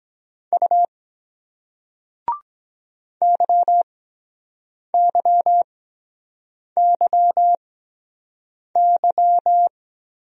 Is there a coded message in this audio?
Yes: Morse code from 0.5 to 9.7 s, heard as a keyed tone, about 23 wpm, spaced as for 7 wpm, 700 Hz. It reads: U YYYY